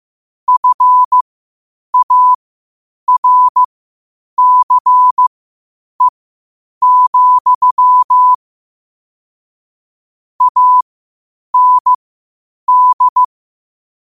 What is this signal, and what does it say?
Morse code, heard as a keyed tone, about 15 wpm, spaced as for 8 wpm, 1000 Hz. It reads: FARCE, AND
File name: qotdfiles / 8wpm